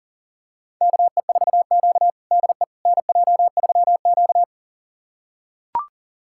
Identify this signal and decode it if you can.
Morse code, heard as a keyed tone, about 40 wpm, 700 Hz. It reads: KE4Q DE NJ3Q
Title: USA Callsign DE Callsign - 4 Characters - 40wpm